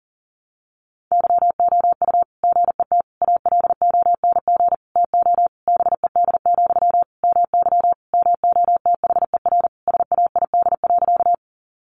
Morse code, heard as a keyed tone, about 40 wpm, 700 Hz. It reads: YOU GET ALONG TO BED, MY MOTHER SAID.
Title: Sentences from Top 700 Words 40wpm